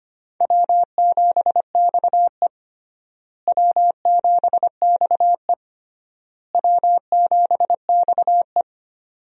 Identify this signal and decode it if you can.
Morse code, heard as a keyed tone, about 25 wpm, 700 Hz. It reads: W7XE W7XE W7XE
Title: USA Call Signs - 4 Characters - Repeated 3x - 25wpm